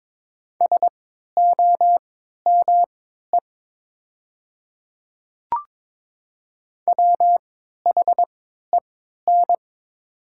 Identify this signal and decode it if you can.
Morse code, heard as a keyed tone, about 22 wpm, spaced as for 12 wpm, 690 Hz. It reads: SOME WHEN
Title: 3x Character Spacing - Top 200 Words - 22wpm